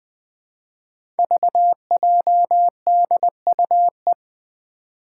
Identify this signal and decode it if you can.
Morse code, heard as a keyed tone, about 20 wpm, 700 Hz. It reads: VJDUE
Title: Five Challenging Letter-Number Combinations 20wpm